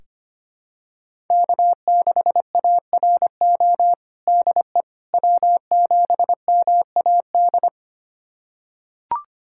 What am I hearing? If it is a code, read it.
Morse code, heard as a keyed tone, about 25 wpm, 700 Hz. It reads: K6ARO DE W7MAD
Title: USA Callsign DE Callsign - 5 Characters - 25wpm